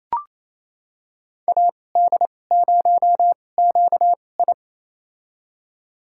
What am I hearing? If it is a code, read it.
Morse code, heard as a keyed tone, about 28 wpm, spaced as for 20 wpm, 700 Hz. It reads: AD0QI
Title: USA Call Signs - 2x Character Spacing - 28wpm